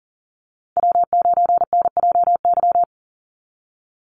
Morse code, heard as a keyed tone, about 41 wpm, 700 Hz. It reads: W9NJY
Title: Call Signs - 40wpm